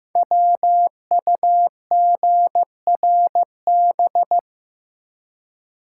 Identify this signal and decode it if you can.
Morse code, heard as a keyed tone, about 15 wpm, 700 Hz. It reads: WUGRB
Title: Code Groups - 5 Letters - 15wpm